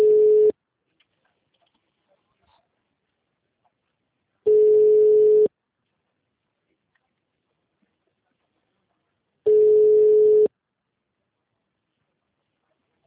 Ringback tone (Europe, 425 Hz).